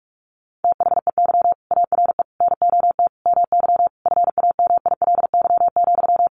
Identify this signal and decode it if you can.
Morse code, heard as a keyed tone, about 44 wpm, 700 Hz. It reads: THEY ARE NOT MY FAMILY,
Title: Sentences from Top 1100 Words 45wpm